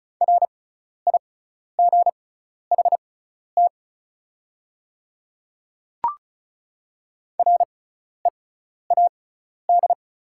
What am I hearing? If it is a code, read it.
Morse code, heard as a keyed tone, about 34 wpm, spaced as for 12 wpm, 690 Hz. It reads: RIGHT READ
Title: 6x Character Spacing - Top 200 Words - 35wpm